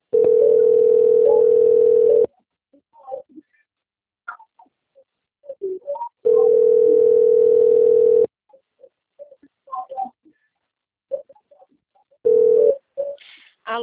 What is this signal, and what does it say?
Ringback tone (North America), 440 + 480 Hz.